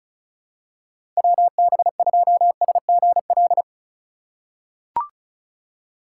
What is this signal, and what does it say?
Morse code, heard as a keyed tone, about 36 wpm, 700 Hz. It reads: WB2SGL
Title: USA Call Signs - 6 Characters - 3x Character Spacing - 35wpm